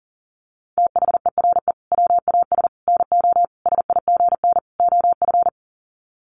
Morse code, heard as a keyed tone, about 40 wpm, 700 Hz. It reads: THERE WAS NO SIGN OF
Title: Sets of 5 Words Encore2 40wpm